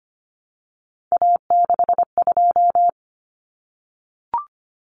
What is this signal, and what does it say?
Morse code, heard as a keyed tone, about 25 wpm, 700 Hz. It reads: A62
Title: Common Cut Numbers - 3-Digit 25wpm